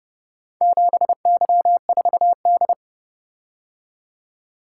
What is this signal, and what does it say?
Morse code, heard as a keyed tone, about 29 wpm, 700 Hz. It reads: 7Y4D